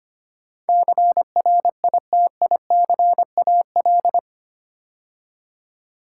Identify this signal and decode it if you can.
Morse code, heard as a keyed tone, about 25 wpm, 700 Hz. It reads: CRITICAL
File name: 8 Letter Words - First 3 Letters Given - 25wpm